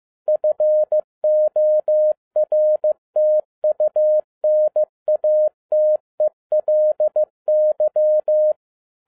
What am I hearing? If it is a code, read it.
Morse code, heard as a keyed tone, about 15 wpm, 610 Hz. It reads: FORTUNATELY
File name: 2011-10-mp3 / n02-FIELD DAY CONTEST QRV FROM SAKAE VILLAGE 15wpm